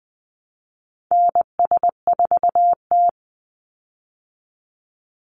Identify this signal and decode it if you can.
Morse code, heard as a keyed tone, about 20 wpm, 700 Hz. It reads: NS4T
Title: Call Signs Repeated 3x - 20wpm